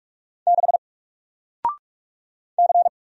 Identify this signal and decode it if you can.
Morse code, heard as a keyed tone, about 46 wpm, 700 Hz. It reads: 6 C
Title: Single Letter-Number Alt 45wpm